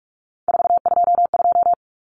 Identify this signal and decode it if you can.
Morse code, heard as a keyed tone, about 44 wpm, 700 Hz. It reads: 422
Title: Numbers - 3-Digit 45wpm